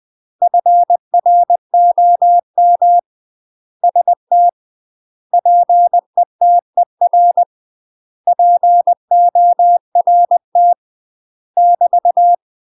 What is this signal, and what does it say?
Morse code, heard as a keyed tone, about 20 wpm, 700 Hz. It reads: FROM ST PETER PORT <BT>
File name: UK MASTER QSOs TRAINING / 20WPM / UK MASTER QSOs 0044